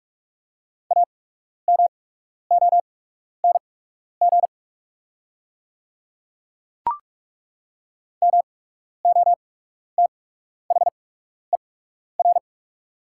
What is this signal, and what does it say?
Morse code, heard as a keyed tone, about 43 wpm, spaced as for 12 wpm, 690 Hz. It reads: AMONG MOTHER
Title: 8x Character Spacing - Top 500 Words - 45wpm